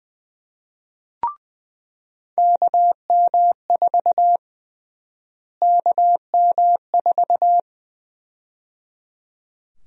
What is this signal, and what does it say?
Morse code, heard as a keyed tone, about 20 wpm, 700 Hz. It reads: KM4 KM4
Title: USA Call Sign Prefixes - Repeated 2x - 20wpm